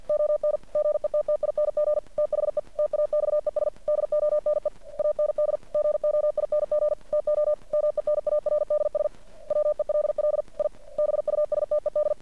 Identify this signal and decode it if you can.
Morse code, heard as a keyed tone, about 51 wpm, 600 Hz. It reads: ON GETTING THE TAXES DONE AND GOING TO MENARDS WELL I BUSTED
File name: archived / CRSnet